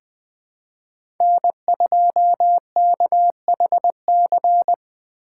Morse code, heard as a keyed tone, about 20 wpm, 700 Hz. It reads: N2KHC
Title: USA Call Signs - 5 Characters - 2x Character Spacing - 20wpm